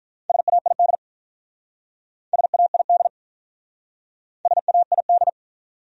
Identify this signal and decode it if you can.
Morse code, heard as a keyed tone, about 45 wpm, 700 Hz. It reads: SAID SAID SAID